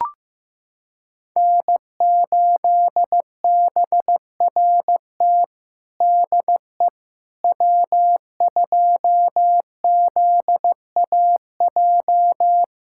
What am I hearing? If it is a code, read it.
Morse code, heard as a keyed tone, about 15 wpm, 700 Hz. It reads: N8BRT DE W2ZAJ
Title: USA Callsign DE Callsign - 5 Characters - 15wpm